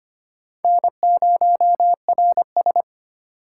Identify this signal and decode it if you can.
Morse code, heard as a keyed tone, about 33 wpm, 700 Hz. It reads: N0RS